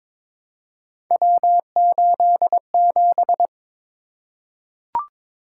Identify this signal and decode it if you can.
Morse code, heard as a keyed tone, about 22 wpm, 700 Hz. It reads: W87
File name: practice / 22wpm / Three Letter-Number Combinations - 22wpm